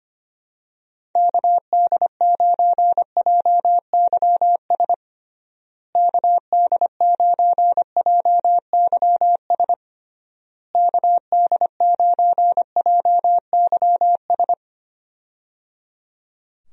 Morse code, heard as a keyed tone, about 25 wpm, 700 Hz. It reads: KD9JYS KD9JYS KD9JYS
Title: USA Call Signs - 6 Characters - Repeated 3x - 25wpm